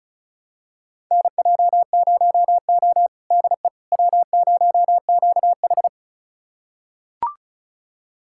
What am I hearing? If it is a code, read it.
Morse code, heard as a keyed tone, about 36 wpm, 700 Hz. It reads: NJ0O DE W0QH
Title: USA Callsign DE Callsign - 4 Characters - 35wpm